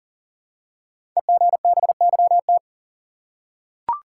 Morse code, heard as a keyed tone, about 39 wpm, 700 Hz. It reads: EGBYT
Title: Code Groups - 5 Letter:Numbers - 40wpm